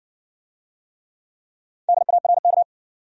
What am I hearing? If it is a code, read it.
Morse code, heard as a keyed tone, about 60 wpm, 700 Hz. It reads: DARK